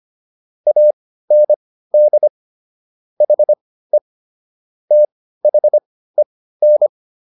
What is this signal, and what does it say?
Morse code, heard as a keyed tone, about 25 wpm, spaced as for 15 wpm, 600 Hz. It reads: AND HE THEN